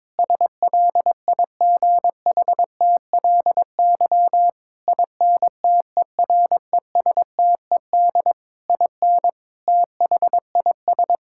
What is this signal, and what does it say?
Morse code, heard as a keyed tone, about 22 wpm, 700 Hz. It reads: SLIGHTLY INTERESTED IN THIS
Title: Sentences from Top 1600 Words 22wpm